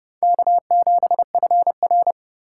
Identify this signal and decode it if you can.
Morse code, heard as a keyed tone, about 39 wpm, 700 Hz. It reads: K7FR